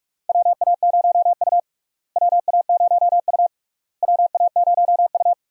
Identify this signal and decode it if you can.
Morse code, heard as a keyed tone, about 46 wpm, 700 Hz. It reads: WA0U WA0U WA0U
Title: USA Call Signs - 4 Characters - Repeated 3x - 45wpm